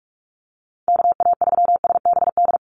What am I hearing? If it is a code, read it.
Morse code, heard as a keyed tone, about 44 wpm, 700 Hz. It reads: KA3SBD
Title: USA Call Signs 45wpm